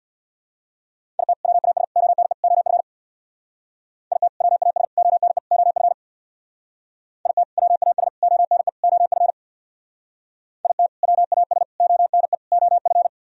Morse code, heard as a keyed tone, about 63 wpm, 690 Hz. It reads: IT WAS ONE OF IT WAS ONE OF IT WAS ONE OF IT WAS ONE OF